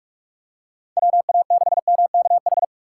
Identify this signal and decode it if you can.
Morse code, heard as a keyed tone, about 47 wpm, 700 Hz. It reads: WA6MKH
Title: USA Call Signs - 6 Characters - 6x Character Spacing - 45wpm